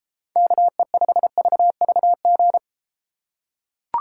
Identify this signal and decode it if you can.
Morse code, heard as a keyed tone, about 32 wpm, 700 Hz. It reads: KE5VVG